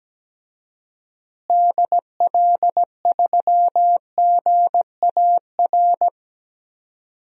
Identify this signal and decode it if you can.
Morse code, heard as a keyed tone, about 17 wpm, 700 Hz. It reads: DL3GAR
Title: DXCC-100 Call Signs - 6 Characters - Repeated 3x - 17wpm